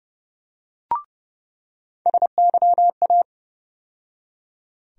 Morse code, heard as a keyed tone, about 30 wpm, 700 Hz. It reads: SYA
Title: USA Call Sign Postfixes - 3 Letters 30wpm